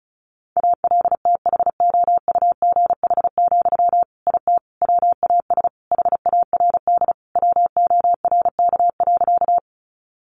Morse code, heard as a keyed tone, about 34 wpm, 700 Hz. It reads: ALTHOUGH, IT WAS HARD WORK.
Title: Sentences from Top 600 Words 35wpm